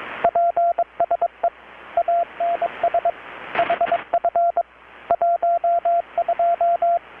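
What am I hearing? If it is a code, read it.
Morse code, heard as a keyed tone, about 22 wpm, 690 Hz, with radio noise. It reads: PSE ANS HF 12
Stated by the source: Morse from KPH's Night of Nights XX maritime radiotelegraphy event, received in Chiapas, Mexico about 3,800 km away on a Kenwood R-600 with a grounded 10 m long-wire antenna.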